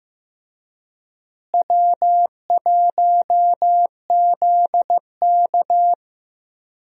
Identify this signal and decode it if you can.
Morse code, heard as a keyed tone, about 14 wpm, 700 Hz. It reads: W1ZK
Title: USA Call Signs - 4 Characters - Repeated 3x - 15wpm